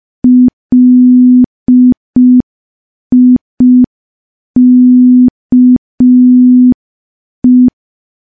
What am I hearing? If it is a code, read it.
Morse code, heard as a keyed tone, about 5 wpm, 250 Hz. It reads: LIKE
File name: morse / morse-training-05WPM